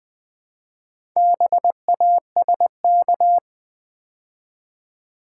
Morse code, heard as a keyed tone, about 20 wpm, 700 Hz. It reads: BASK
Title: Build A Word - 2 Letters At A Time - 6-Letter Words - 20wpm